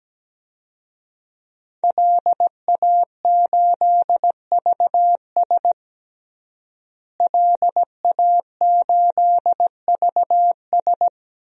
Morse code, heard as a keyed tone, about 17 wpm, 700 Hz. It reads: LA8VS LA8VS
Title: DXCC-100 Call Signs - 5 Characters - Repeated 3x - 17wpm